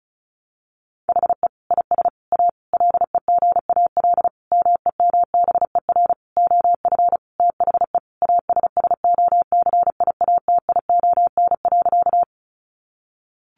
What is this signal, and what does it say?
Morse code, heard as a keyed tone, about 34 wpm, 700 Hz. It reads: HE IS A LEGAL MEMBER OF THE ASSOCIATION.
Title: Sentences from Top 1200 Words 35wpm